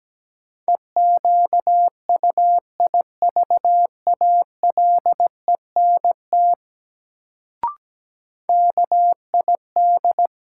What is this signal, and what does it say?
Morse code, heard as a keyed tone, about 17 wpm, 700 Hz. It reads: EQUIVALENT KID
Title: Sentences from Top 1600 Words - New Word Drill - 17wpm